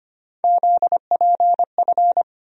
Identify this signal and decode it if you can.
Morse code, heard as a keyed tone, about 25 wpm, 700 Hz. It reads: ZPF
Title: USA Call Sign Postfixes - 3 Letters 25wpm